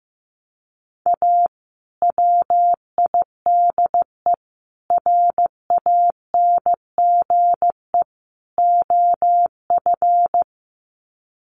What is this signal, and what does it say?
Morse code, heard as a keyed tone, about 15 wpm, 700 Hz. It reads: A WIDE RANGE OF
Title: Sets of 4 Words 15wpm